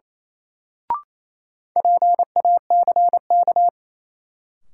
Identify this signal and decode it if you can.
Morse code, heard as a keyed tone, about 28 wpm, 700 Hz. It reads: PACK
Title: Sentences from Top 1300 Words - New Word Drill Alt - 28wpm